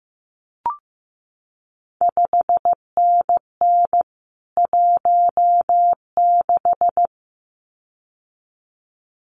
Morse code, heard as a keyed tone, about 15 wpm, 700 Hz. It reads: 5NN 16
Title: CQ WW DX Contest Exchange - 15wpm